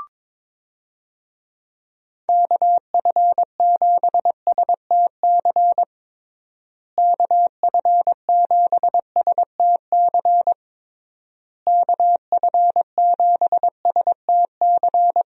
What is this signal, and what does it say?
Morse code, heard as a keyed tone, about 22 wpm, 700 Hz. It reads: KF7STC KF7STC KF7STC
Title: USA Call Signs - 6 Characters - Repeated 3x - 22wpm